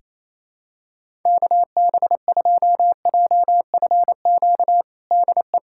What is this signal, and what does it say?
Morse code, heard as a keyed tone, about 28 wpm, 700 Hz. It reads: KB2JFQ DE
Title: USA Callsign DE Callsign - 6 Characters - 28wpm